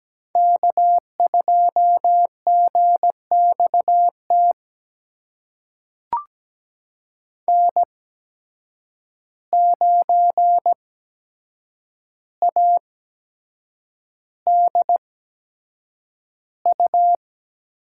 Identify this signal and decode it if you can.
Morse code, heard as a keyed tone, about 17 wpm, spaced as for 7 wpm, 700 Hz. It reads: K2GXT N9ADU
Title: USA Call Signs - 5 Characters - 8x Character Spacing - 17wpm